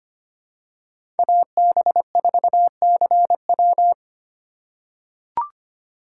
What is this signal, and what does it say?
Morse code, heard as a keyed tone, about 25 wpm, 700 Hz. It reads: AB4CW
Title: USA Call Signs - 5 Characters - 2x Character Spacing - 25wpm